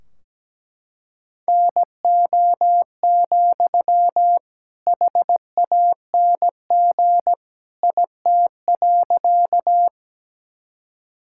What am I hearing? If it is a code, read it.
Morse code, heard as a keyed tone, about 17 wpm, 700 Hz. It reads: NO, HANG IT.
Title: Sentences from Top 1600 Words 17wpm